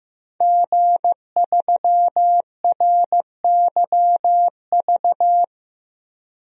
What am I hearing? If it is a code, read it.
Morse code, heard as a keyed tone, about 15 wpm, 700 Hz. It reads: G3RYV
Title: DXCC-100 Call Signs - 5 Characters - Repeated 2x - 15wpm